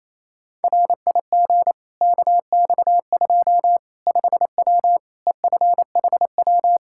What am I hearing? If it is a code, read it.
Morse code, heard as a keyed tone, about 27 wpm, 700 Hz. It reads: RIG KX2 5W EFHW
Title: Practice QSO - Exchange 2 - Info - 28wpm